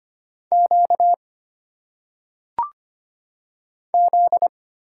Morse code, heard as a keyed tone, about 25 wpm, 700 Hz. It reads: Q Z
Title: Single Letters - Phonetic Pronunciation 25wpm